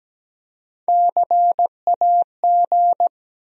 Morse code, heard as a keyed tone, about 17 wpm, 700 Hz. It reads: CAG
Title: USA Call Sign Postfixes - 3 Letters 17wpm